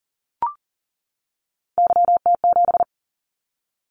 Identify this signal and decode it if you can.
Morse code, heard as a keyed tone, about 40 wpm, 700 Hz. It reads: YT7
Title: Callsign Prefixes for Top 25 DXCC - 40wpm